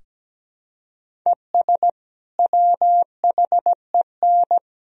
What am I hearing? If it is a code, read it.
Morse code, heard as a keyed tone, about 17 wpm, 700 Hz. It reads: ES WHEN
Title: Sentences from Top 100 Words and Abbreviations 17wpm